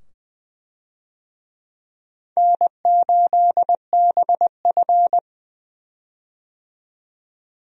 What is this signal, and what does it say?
Morse code, heard as a keyed tone, about 20 wpm, 700 Hz. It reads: N8BF
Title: USA Call Signs - 4 Characters - 20wpm